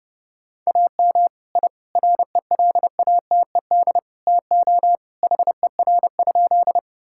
Morse code, heard as a keyed tone, about 32 wpm, 700 Hz. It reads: AM I RELATED TO HER?
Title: Sentences from Top 1000 Words - Review Part 5 - 30wpm